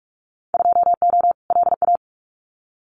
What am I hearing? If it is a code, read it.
Morse code, heard as a keyed tone, about 44 wpm, 700 Hz. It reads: 2O LA
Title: Winter Field Day Exchanges 45wpm